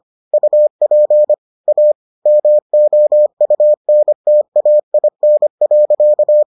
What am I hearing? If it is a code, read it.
Morse code, heard as a keyed tone, about 25 wpm, 600 Hz. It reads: UP A MOUNTAIN.